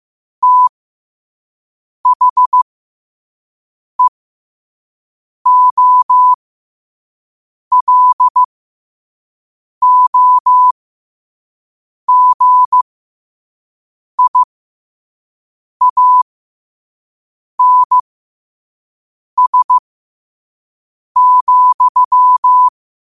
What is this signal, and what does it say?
Morse code, heard as a keyed tone, about 15 wpm, spaced as for 5 wpm, 1000 Hz. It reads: THEOLOGIANS,
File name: qotdfiles / 5wpm